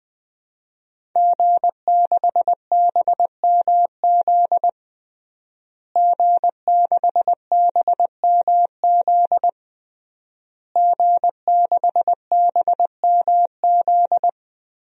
Morse code, heard as a keyed tone, about 20 wpm, 700 Hz. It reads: G6BMZ G6BMZ G6BMZ
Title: DXCC-100 Call Signs - 5 Characters - Repeated 3x - 20wpm